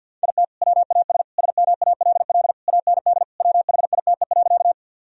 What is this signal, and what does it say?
Morse code, heard as a keyed tone, about 46 wpm, 700 Hz. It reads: IT WAS SMALL AND WHITE.
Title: Sentences from Top 1000 Words - Review Part 1 - 50wpm